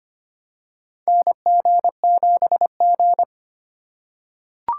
Morse code, heard as a keyed tone, about 25 wpm, 700 Hz. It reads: NG7G
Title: USA Call Signs - 2x Character Spacing - 25wpm